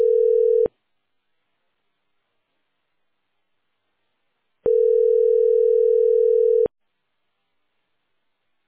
Ringback tone (North America), 440 + 480 Hz.